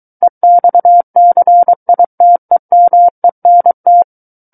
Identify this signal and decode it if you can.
Morse code, heard as a keyed tone, about 23 wpm, 700 Hz. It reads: EXCITEMENT
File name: Book One / 14-Chapter 14 - 23 WPM